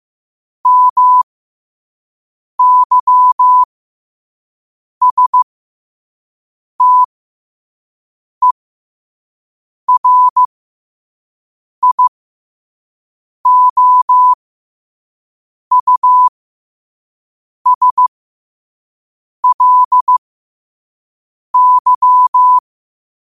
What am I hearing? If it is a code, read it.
Morse code, heard as a keyed tone, about 15 wpm, spaced as for 5 wpm, 1000 Hz. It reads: MYSTERIOUSLY
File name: qotdfiles / 5wpm